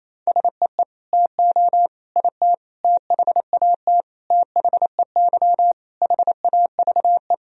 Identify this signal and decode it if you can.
Morse code, heard as a keyed tone, about 28 wpm, 700 Hz. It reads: SEE TO IT THAT THEY HAVE
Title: Sentences from Top 1000 Words - Review Part 2 - 28wpm